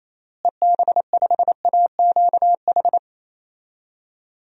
Morse code, heard as a keyed tone, about 28 wpm, 700 Hz. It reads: EB5AQH
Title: DXCC-100 Call Signs - 6 Characters - Repeated 3x - 28wpm